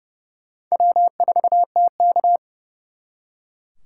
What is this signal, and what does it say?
Morse code, heard as a keyed tone, about 38 wpm, 700 Hz. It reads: W4TK